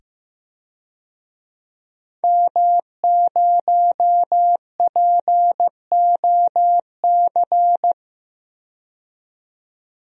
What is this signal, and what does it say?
Morse code, heard as a keyed tone, about 15 wpm, 700 Hz. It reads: M0POC
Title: DXCC-100 Call Signs - 5 Characters - 15wpm